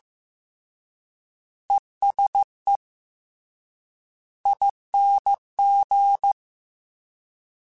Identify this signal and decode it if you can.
Morse code, heard as a keyed tone, about 15 wpm, 800 Hz. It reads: ESE ING
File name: General / General-mp3s / ESE ING @13